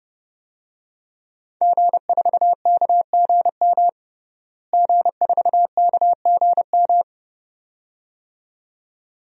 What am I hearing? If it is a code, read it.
Morse code, heard as a keyed tone, about 30 wpm, 700 Hz. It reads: G4KGM G4KGM
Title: DXCC-100 Call Signs - 5 Characters - Repeated 2x - 30wpm